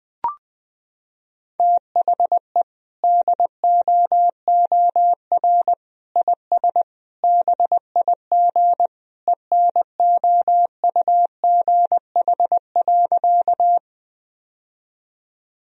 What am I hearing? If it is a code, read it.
Morse code, heard as a keyed tone, about 20 wpm, 700 Hz. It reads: THE DOOR IS BIG ENOUGH.
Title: Sentences from Top 1000 Words - Review Part 1 - 20wpm